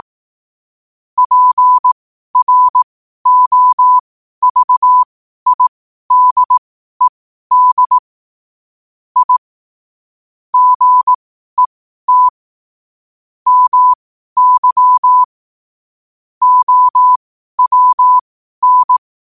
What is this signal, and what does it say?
Morse code, heard as a keyed tone, about 18 wpm, spaced as for 12 wpm, 1000 Hz. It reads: PROVIDED I GET MY OWN
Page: New quotes every day in morse code at 12 Words per minute.